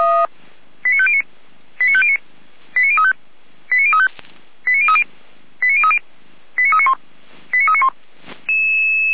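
Five-tone selective call (ZVEI), heard as quick runs of five tones, tones of about 70 ms. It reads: 89509 89509 89035 89035 89030 89030 89391 89391
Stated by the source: ZVEI five-tone call 89187 received by SDR.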